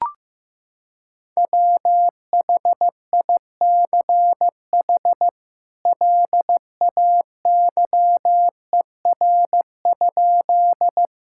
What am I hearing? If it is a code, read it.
Morse code, heard as a keyed tone, about 15 wpm, 700 Hz. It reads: WHICH LAYER?
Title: Sentences from Top 1600 Words 15wpm